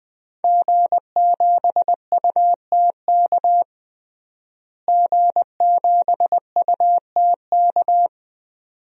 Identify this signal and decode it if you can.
Morse code, heard as a keyed tone, about 20 wpm, 700 Hz. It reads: G7UTK G7UTK
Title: DXCC-100 Call Signs - 5 Characters - Repeated 2x - 20wpm